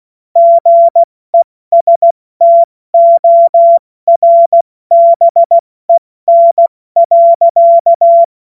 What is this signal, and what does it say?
Morse code, heard as a keyed tone, about 16 wpm, 680 Hz. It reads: GESTORBEN.
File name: Morsekurs / Witze / Witz 04. Juli 2023 70 F80